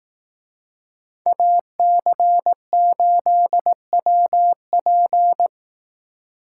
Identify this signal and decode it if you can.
Morse code, heard as a keyed tone, about 18 wpm, 700 Hz. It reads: AC8WP
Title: Speed Racing - USA 2x2 Call Signs 15wpm